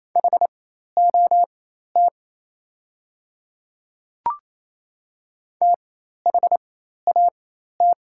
Morse code, heard as a keyed tone, about 27 wpm, spaced as for 13 wpm, 690 Hz. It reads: HOT THAT
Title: 4x Character Spacing - Top 100 Words - 28wpm